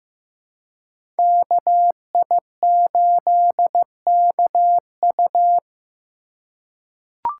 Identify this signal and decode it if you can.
Morse code, heard as a keyed tone, about 15 wpm, 700 Hz. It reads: KI8KU
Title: USA Call Signs - 5 Characters - 3x Character Spacing - 15wpm